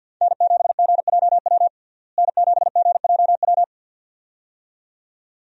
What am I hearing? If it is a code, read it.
Morse code, heard as a keyed tone, about 51 wpm, 700 Hz. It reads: N7GJW N7GJW
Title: USA Call Signs - 5 Characters - Repeated 2x - 50wpm